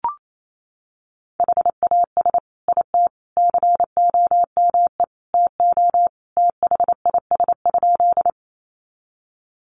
Morse code, heard as a keyed tone, about 28 wpm, 700 Hz. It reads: HAS IT COME TO THIS?
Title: Sentences from Top 100 Words 28wpm